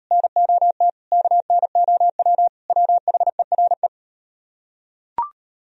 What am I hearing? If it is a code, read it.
Morse code, heard as a keyed tone, about 37 wpm, 700 Hz. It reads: NOT KNOW WHERE